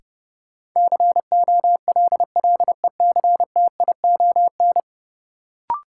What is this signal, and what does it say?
Morse code, heard as a keyed tone, about 30 wpm, 700 Hz. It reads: COLLECTION
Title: Sentences from Top 1200 Words - New Word Drill Alt - 30wpm